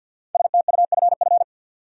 Morse code, heard as a keyed tone, about 52 wpm, 700 Hz. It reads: STUFF
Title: Sentences from Top 1400 Words - New Word Drill - 50wpm